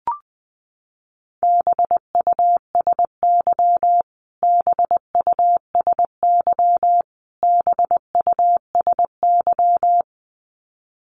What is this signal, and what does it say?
Morse code, heard as a keyed tone, about 20 wpm, 700 Hz. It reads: BUSY BUSY BUSY
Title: Top 500 Words Repeated 20wpm